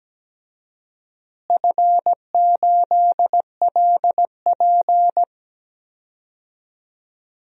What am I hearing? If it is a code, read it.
Morse code, heard as a keyed tone, about 17 wpm, 700 Hz. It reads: F8LP
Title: DXCC-100 Call Signs - 4 Characters - Repeated 2x - 17wpm